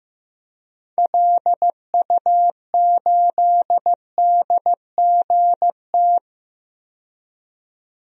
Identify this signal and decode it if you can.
Morse code, heard as a keyed tone, about 16 wpm, 700 Hz. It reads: LU8DGT
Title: DXCC-100 Call Signs - 6 Characters - 15wpm